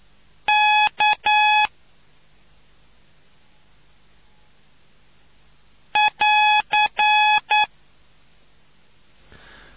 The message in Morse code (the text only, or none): K <AR>